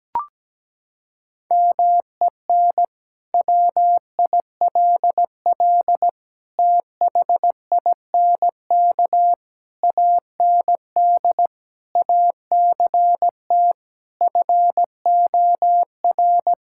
Morse code, heard as a keyed tone, about 17 wpm, 700 Hz. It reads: MEN WILL THINK AND ACT FOR
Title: Sentences from Top 1000 Words - Review Part 5 - 17wpm